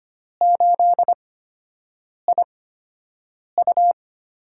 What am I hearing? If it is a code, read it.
Morse code, heard as a keyed tone, about 25 wpm, spaced as for 7 wpm, 700 Hz. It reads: 8IU
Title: USA Call Signs - 5 Characters - 8x Character Spacing - 25wpm